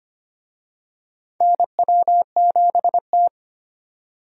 Morse code, heard as a keyed tone, about 25 wpm, 700 Hz. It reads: NW7T
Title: USA Call Signs - 4 Characters - 25wpm